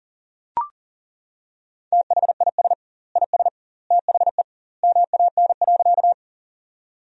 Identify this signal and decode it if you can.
Morse code, heard as a keyed tone, about 40 wpm, 700 Hz. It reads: THIS IS THE MAN.
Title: 2x Word-Spacing - Sentences from Top 200 Words - 40wpm